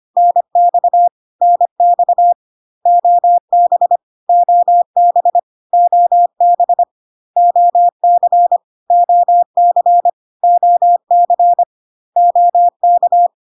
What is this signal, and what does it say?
Morse code, heard as a keyed tone, about 25 wpm, 700 Hz. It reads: NX NX OB OB OB OC OC OC OK
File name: CW abbreviations alphabetical repeated 3 times 25 25wpm